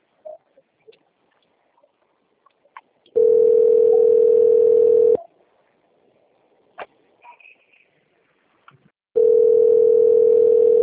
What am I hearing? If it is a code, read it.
Ringback tone (North America), 440 + 480 Hz.